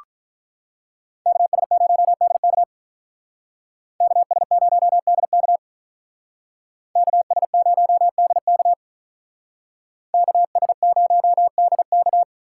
Morse code, heard as a keyed tone, about 46 wpm, 700 Hz. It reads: KS0DK KS0DK KS0DK KS0DK